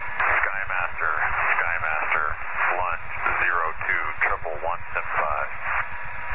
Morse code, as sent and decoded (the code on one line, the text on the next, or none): none